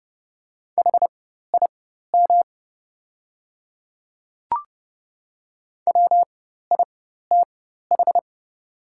Morse code, heard as a keyed tone, about 30 wpm, spaced as for 14 wpm, 700 Hz. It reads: HIM WITH
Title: 4x Character Spacing - Top 100 Words - 30wpm